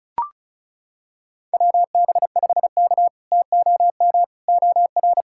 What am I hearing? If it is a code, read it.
Morse code, heard as a keyed tone, about 34 wpm, 700 Hz. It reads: WB5K TOM OR
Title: K1USN SST QSO Practice - 35wpm